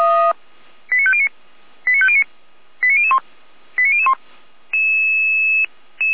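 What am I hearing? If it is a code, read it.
Five-tone selective call (ZVEI), heard as quick runs of five tones, tones of about 70 ms. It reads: 89509 89509 89001 89001
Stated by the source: ZVEI five-tone selective call received on SDR.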